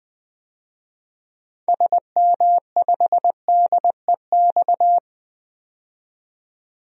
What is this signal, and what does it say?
Morse code, heard as a keyed tone, about 20 wpm, 700 Hz. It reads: SM5DEX
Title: DXCC-100 Call Signs - 6 Characters - 20wpm